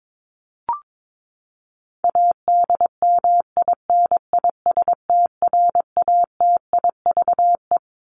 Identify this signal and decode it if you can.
Morse code, heard as a keyed tone, about 22 wpm, 700 Hz. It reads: ADMINISTRATIVE
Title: Top 1501-1750 Words 22wpm